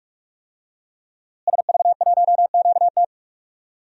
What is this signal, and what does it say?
Morse code, heard as a keyed tone, about 41 wpm, 700 Hz. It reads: SV1QT
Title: DXCC-100 Call Signs - 5 Characters - Repeated 3x - 45wpm